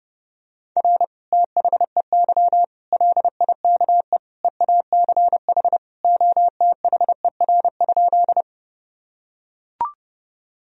Morse code, heard as a keyed tone, about 30 wpm, 700 Hz. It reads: R THEY LIKE EACH OTHER?
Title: Sentences from Top 200 Words and Abbreviations 30wpm